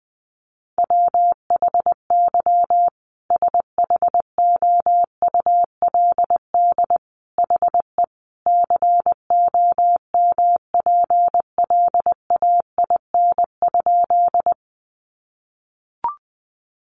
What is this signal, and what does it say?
Morse code, heard as a keyed tone, about 19 wpm, 700 Hz. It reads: WHY SHOULD HE COMPLAIN?
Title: Sentences from Top 1300 Words 20wpm